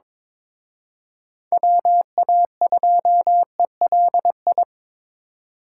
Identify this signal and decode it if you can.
Morse code, heard as a keyed tone, about 25 wpm, 700 Hz. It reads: WA2ELI
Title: Speed Racing - USA 2x3 Call Signs 22wpm